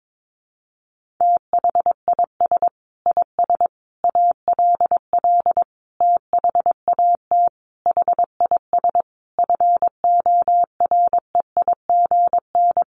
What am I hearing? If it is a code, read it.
Morse code, heard as a keyed tone, about 22 wpm, 700 Hz. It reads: THIS IS ALL THAT HIS FOREIGN
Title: Sentences from Top 1400 Words 22wpm